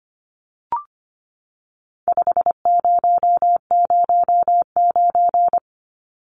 Morse code, heard as a keyed tone, about 25 wpm, 700 Hz. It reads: 5009
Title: Numbers - 4-Digit 25wpm